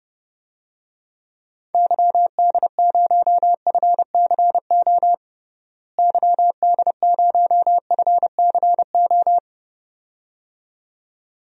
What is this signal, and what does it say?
Morse code, heard as a keyed tone, about 29 wpm, 700 Hz. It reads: YD0FCO YD0FCO
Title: DXCC-100 Call Signs - 6 Characters - Repeated 2x - 30wpm